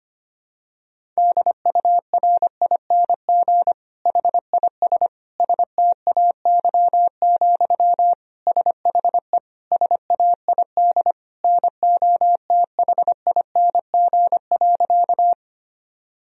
Morse code, heard as a keyed tone, about 25 wpm, 700 Hz. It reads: DURING HIS STAY, SHE SAID NOTHING.
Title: Sentences from Top 1000 Words - Review Part 2 - 25wpm